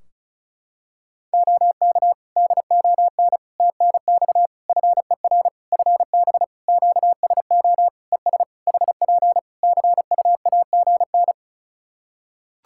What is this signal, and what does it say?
Morse code, heard as a keyed tone, about 36 wpm, 700 Hz. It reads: OK DON TNX FER FB QSO ES HP CUAGN
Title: Practice QSO - Exchange 4 - Ending - 35wpm